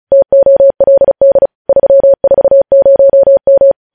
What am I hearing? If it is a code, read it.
Morse code, heard as a keyed tone, about 36 wpm, 550 Hz. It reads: TOLD 340M